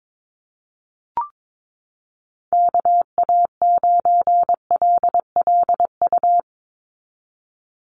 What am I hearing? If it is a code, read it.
Morse code, heard as a keyed tone, about 22 wpm, 700 Hz. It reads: KA9LLU
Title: USA Call Signs 22wpm